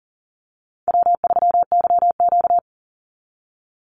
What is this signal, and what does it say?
Morse code, heard as a keyed tone, about 40 wpm, 700 Hz. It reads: W3YQ
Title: Easy Call Signs Repeated 3x - 40wpm